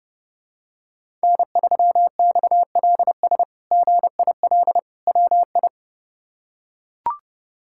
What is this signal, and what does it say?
Morse code, heard as a keyed tone, about 30 wpm, 700 Hz. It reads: N3XLS GIL WI
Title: K1USN SST QSO Practice - 30wpm